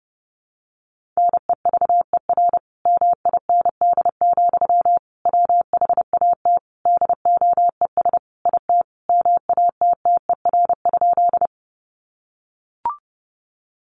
Morse code, heard as a keyed tone, about 30 wpm, 700 Hz. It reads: NEVER MIND, WHAT DOES IT MATTER?
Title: Sentences from Top 700 Words 30wpm